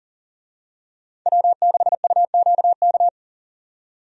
Morse code, heard as a keyed tone, about 40 wpm, 700 Hz. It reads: W6UQK
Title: USA Call Signs - 5 Characters - 40wpm